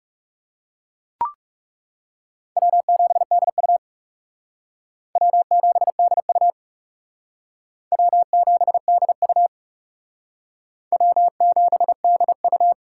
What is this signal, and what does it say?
Morse code, heard as a keyed tone, about 40 wpm, 700 Hz. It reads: W7DU W7DU W7DU W7DU